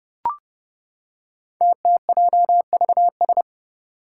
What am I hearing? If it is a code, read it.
Morse code, heard as a keyed tone, about 29 wpm, 700 Hz. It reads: TTJVS
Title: Five Challenging Letter-Number Combinations 30wpm